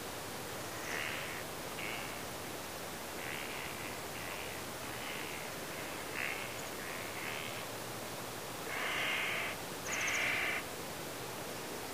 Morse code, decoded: none